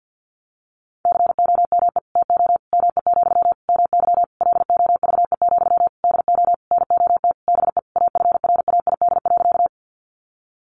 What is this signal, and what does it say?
Morse code, heard as a keyed tone, about 50 wpm, 700 Hz. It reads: COME TO ME, MY LOVE, DO NOT BE AFRAID.
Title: Sentences from Top 1500 Words 50wpm